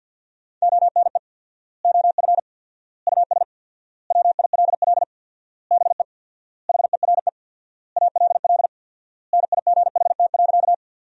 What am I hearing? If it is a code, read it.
Morse code, heard as a keyed tone, about 50 wpm, 700 Hz. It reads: ONE OF US WILL BE HERE ALL NIGHT.
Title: 4x Word-Spacing - Sentences from Top 300 Words - 50wpm